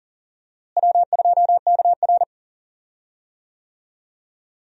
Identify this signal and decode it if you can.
Morse code, heard as a keyed tone, about 40 wpm, 700 Hz. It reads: W2KR
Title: USA Call Signs - 4 Characters - Repeated 2x - 40wpm